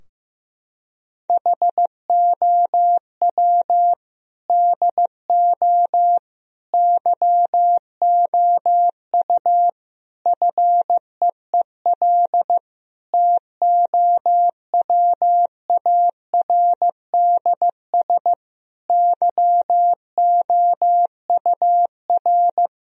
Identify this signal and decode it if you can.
Morse code, heard as a keyed tone, about 15 wpm, 700 Hz. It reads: HOW DO YOU FEEL TOWARDS YOUR
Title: Sentences from Top 1000 Words - Review Part 2 - 15wpm